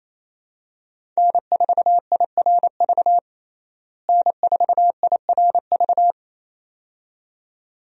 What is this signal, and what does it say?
Morse code, heard as a keyed tone, about 28 wpm, 700 Hz. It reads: N4IRV N4IRV
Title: USA Call Signs - 5 Characters - Repeated 2x - 28wpm